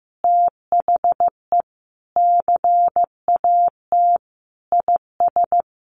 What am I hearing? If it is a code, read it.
Morse code, heard as a keyed tone, about 15 wpm, 700 Hz. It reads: THE CAT IS
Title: Additional Easy Phrases 15wpm